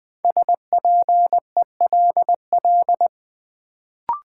Morse code, heard as a keyed tone, about 20 wpm, 700 Hz. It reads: SPELL
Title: Sentences from Top 1000 Words - Review Part 1 - New Word Drill - 20wpm